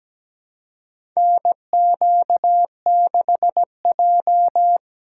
Morse code, heard as a keyed tone, about 17 wpm, 700 Hz. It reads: NQ6J
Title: USA Call Signs - 4 Characters - 3x Character Spacing - 17wpm